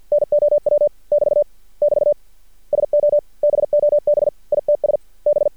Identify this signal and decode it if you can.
Morse code, heard as a keyed tone, about 48 wpm, 590 Hz. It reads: NOW <BT> <BT> SO BOB ITS B
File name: archived / CRSnet